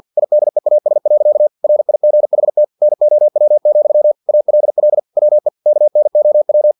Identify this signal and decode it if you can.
Morse code, heard as a keyed tone, about 50 wpm, 600 Hz. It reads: IDEAS. RIGHT NOW, ALL WE KNOW
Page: Never 49wpm